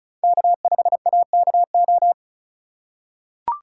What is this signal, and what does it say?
Morse code, heard as a keyed tone, about 36 wpm, 700 Hz. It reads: K5AKO
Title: USA Call Signs - 5 Characters - 3x Character Spacing - 35wpm